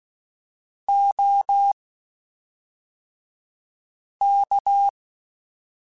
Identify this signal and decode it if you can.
Morse code, heard as a keyed tone, about 16 wpm, spaced as for 3 wpm, 800 Hz. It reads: OK
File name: (spaced) / QSOs Spaced-mp3s / WD0FBT 1 @5 sp